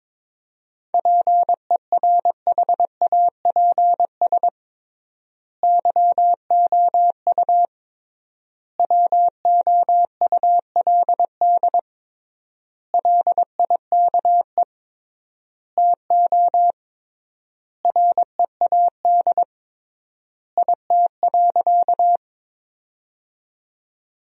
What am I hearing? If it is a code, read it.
Morse code, heard as a keyed tone, about 22 wpm, 700 Hz. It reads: PERHAPS YOU WOULD LIKE TO READ IT.
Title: 3x Word-Spacing - Sentences from Top 500 Words - 22wpm